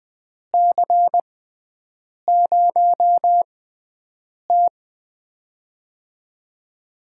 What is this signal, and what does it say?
Morse code, heard as a keyed tone, about 20 wpm, spaced as for 7 wpm, 700 Hz. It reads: C0T
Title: USA Call Signs - 6x Character Spacing - 20wpm